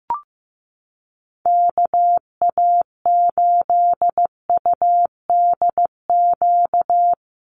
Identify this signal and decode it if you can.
Morse code, heard as a keyed tone, about 15 wpm, 700 Hz. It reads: KA8UDQ
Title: USA Call Signs Repeated 3x 15wpm